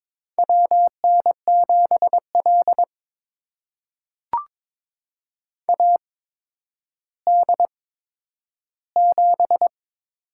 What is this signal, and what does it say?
Morse code, heard as a keyed tone, about 22 wpm, spaced as for 11 wpm, 700 Hz. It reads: WN7L AD7
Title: USA Call Signs - 4 Characters - 8x Character Spacing - 22wpm